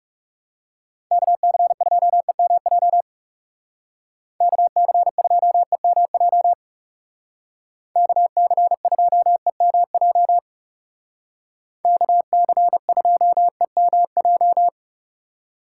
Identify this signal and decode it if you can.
Morse code, heard as a keyed tone, about 35 wpm, 700 Hz. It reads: KC2EMJ KC2EMJ KC2EMJ KC2EMJ
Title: Speed Racing - USA 2x3 Call Signs 30wpm